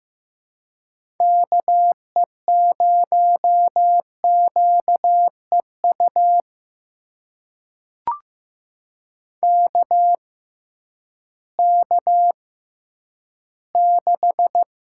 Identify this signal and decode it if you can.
Morse code, heard as a keyed tone, about 15 wpm, spaced as for 10 wpm, 700 Hz. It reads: KE0QEU KK6
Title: USA Call Signs - 6 Characters - 6x Character Spacing - 15wpm